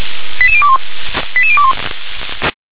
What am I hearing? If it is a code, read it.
Five-tone selective call (ZVEI), heard as quick runs of five tones, tones of about 70 ms. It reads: 88021 88021